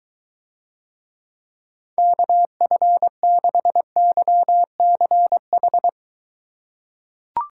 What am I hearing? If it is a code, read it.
Morse code, heard as a keyed tone, about 22 wpm, 700 Hz. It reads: KF6YCH